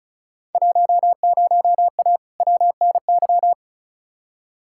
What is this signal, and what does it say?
Morse code, heard as a keyed tone, about 36 wpm, 700 Hz. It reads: 10A WNY
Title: Field Day Exchanges - 35wpm